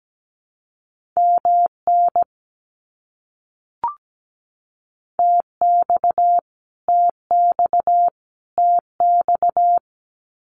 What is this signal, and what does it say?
Morse code, heard as a keyed tone, about 17 wpm, 700 Hz. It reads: MN TX TX TX
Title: US States Repeated 3x 17wpm